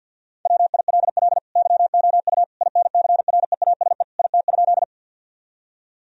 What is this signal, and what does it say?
Morse code, heard as a keyed tone, about 50 wpm, 700 Hz. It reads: WILL YOU INCREASE IT?
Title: Sentences from Top 1000 Words - Review Part 4 - 50wpm